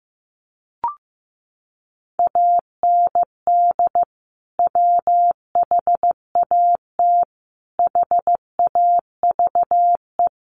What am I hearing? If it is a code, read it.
Morse code, heard as a keyed tone, about 14 wpm, 700 Hz. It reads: AND WHAT HAVE
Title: Sentences from Top 500 Words 15wpm